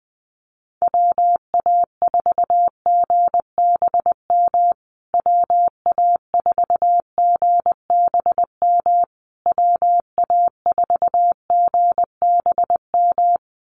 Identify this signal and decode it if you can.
Morse code, heard as a keyed tone, about 19 wpm, 700 Hz. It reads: WA4GBM WA4GBM WA4GBM
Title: USA Call Signs Repeated 3x 20wpm